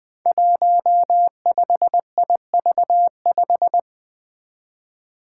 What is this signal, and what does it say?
Morse code, heard as a keyed tone, about 20 wpm, 700 Hz. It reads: 15IV5
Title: Five Challenging Letter-Number Combinations 20wpm